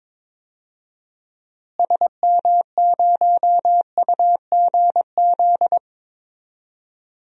Morse code, heard as a keyed tone, about 22 wpm, 700 Hz. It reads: SM0UGZ
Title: DXCC-100 Call Signs - 6 Characters - Repeated 3x - 22wpm